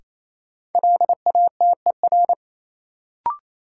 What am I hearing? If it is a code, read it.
Morse code, heard as a keyed tone, about 28 wpm, 700 Hz. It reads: LATER
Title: Sentences from Top 700 Words - New Word Drill - 28wpm